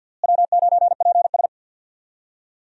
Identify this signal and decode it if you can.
Morse code, heard as a keyed tone, about 52 wpm, 700 Hz. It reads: W9PS
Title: USA Call Signs - 4 Characters - Repeated 3x - 50wpm